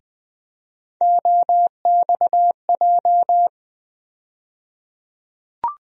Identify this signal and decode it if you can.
Morse code, heard as a keyed tone, about 20 wpm, 700 Hz. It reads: OXJ
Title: USA Call Sign Postfixes - 3 Letters 20wpm